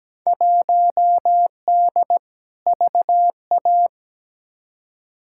Morse code, heard as a keyed tone, about 17 wpm, 700 Hz. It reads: 1D VA
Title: Field Day Exchanges - 17wpm